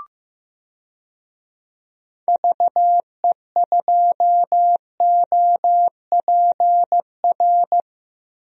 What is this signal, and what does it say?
Morse code, heard as a keyed tone, about 15 wpm, 700 Hz. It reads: VE2OPR
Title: DXCC-100 Call Signs - 6 Characters - 15wpm